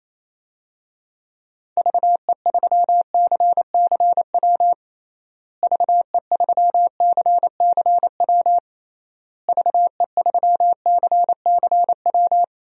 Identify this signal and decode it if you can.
Morse code, heard as a keyed tone, about 27 wpm, 700 Hz. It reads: VE3CCW VE3CCW VE3CCW
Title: DXCC-100 Call Signs - 6 Characters - Repeated 3x - 28wpm